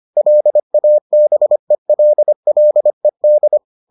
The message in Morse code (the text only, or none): LABELLED